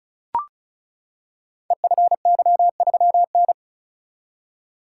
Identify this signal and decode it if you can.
Morse code, heard as a keyed tone, about 36 wpm, 700 Hz. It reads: EFY3N
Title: Code Groups - 5 Letter:Numbers - 35wpm